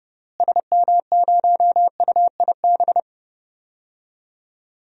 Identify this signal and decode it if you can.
Morse code, heard as a keyed tone, about 30 wpm, 700 Hz. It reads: SM0UIB
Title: DXCC-100 Call Signs - 6 Characters - Repeated 3x - 30wpm